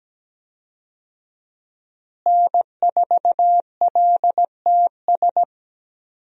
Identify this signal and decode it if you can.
Morse code, heard as a keyed tone, about 17 wpm, 700 Hz. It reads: N4LTS
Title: USA Call Signs - 5 Characters - Repeated 2x - 17wpm